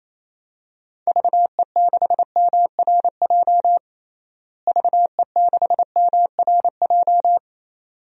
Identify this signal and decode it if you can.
Morse code, heard as a keyed tone, about 28 wpm, 700 Hz. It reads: VE6MRJ VE6MRJ
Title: DXCC-100 Call Signs - 6 Characters - Repeated 2x - 28wpm